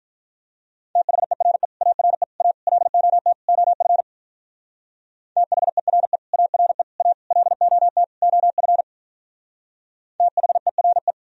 Morse code, heard as a keyed tone, about 44 wpm, 700 Hz. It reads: THERE ARE A LOT OF THERE ARE A LOT OF THERE
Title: Speed-Racing - Sets of 5 Words 35wpm